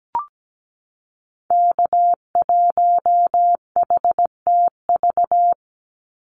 Morse code, heard as a keyed tone, about 17 wpm, 700 Hz. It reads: K1HTV
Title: Call Signs - 17wpm